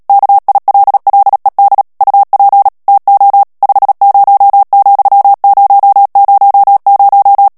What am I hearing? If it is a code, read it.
Morse code, heard as a keyed tone, about 37 wpm, 800 Hz. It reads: KILLED UP TO 50,000
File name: morse mp3 / ANGLAIS texte clair 2 vitesse 35 mots minute